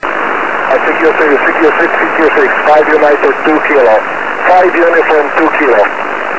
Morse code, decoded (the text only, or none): none